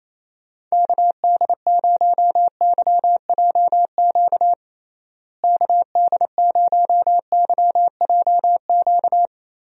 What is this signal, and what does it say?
Morse code, heard as a keyed tone, about 28 wpm, 700 Hz. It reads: KD0YJQ KD0YJQ
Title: USA Call Signs - 6 Characters - Repeated 3x - 28wpm